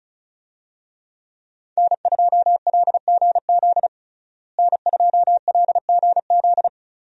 Morse code, heard as a keyed tone, about 36 wpm, 700 Hz. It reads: N2LGZ N2LGZ
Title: USA Call Signs - 5 Characters - Repeated 3x - 35wpm